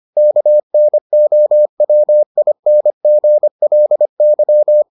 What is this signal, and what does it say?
Morse code, heard as a keyed tone, about 25 wpm, 600 Hz. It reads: KNOWINGLY